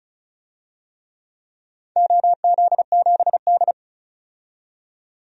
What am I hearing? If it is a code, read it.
Morse code, heard as a keyed tone, about 36 wpm, 700 Hz. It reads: OZ7D
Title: DXCC-100 Call Signs - 4 Characters - 35wpm